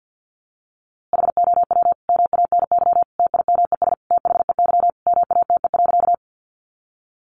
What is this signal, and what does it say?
Morse code, heard as a keyed tone, about 50 wpm, 700 Hz. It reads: HOW MANY TIMES THEY MATE.
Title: A Sound Of Thunder 50wpm